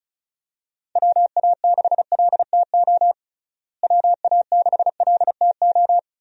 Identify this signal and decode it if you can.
Morse code, heard as a keyed tone, about 36 wpm, 700 Hz. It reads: WA6LTO WA6LTO
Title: USA Call Signs - 6 Characters - Repeated 2x - 35wpm